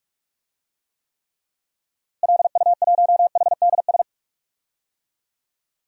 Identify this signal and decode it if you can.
Morse code, heard as a keyed tone, about 46 wpm, 700 Hz. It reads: LU1HDS
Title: DXCC-100 Call Signs - 6 Characters - Repeated 3x - 45wpm